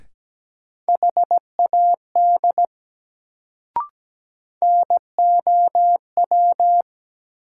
Morse code, heard as a keyed tone, about 17 wpm, 700 Hz. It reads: HAD NOW
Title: Sets of 2 Words - New Word Drill - 17wpm